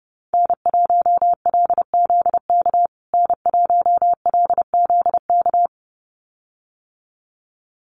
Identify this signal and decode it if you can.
Morse code, heard as a keyed tone, about 30 wpm, 700 Hz. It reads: N1LZK N1LZK
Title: USA Call Signs Repeated 2x 30wpm